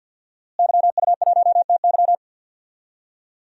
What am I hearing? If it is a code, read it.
Morse code, heard as a keyed tone, about 49 wpm, 700 Hz. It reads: YU1TY
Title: DXCC-100 Call Signs - 5 Characters - Repeated 2x - 50wpm